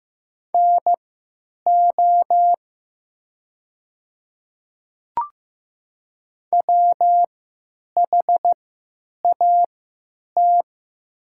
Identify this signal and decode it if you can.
Morse code, heard as a keyed tone, about 15 wpm, spaced as for 9 wpm, 690 Hz. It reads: NO WHAT
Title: 3x Character Spacing - Top 100 Words - 15wpm